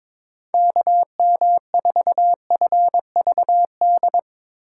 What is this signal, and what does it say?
Morse code, heard as a keyed tone, about 22 wpm, 700 Hz. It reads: KM4FVD
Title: USA Call Signs - 6 Characters - 3x Character Spacing - 22wpm